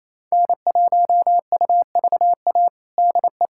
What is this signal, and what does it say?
Morse code, heard as a keyed tone, about 28 wpm, 700 Hz. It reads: N1UVA DE
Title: USA Callsign DE Callsign - 5 Characters - 28wpm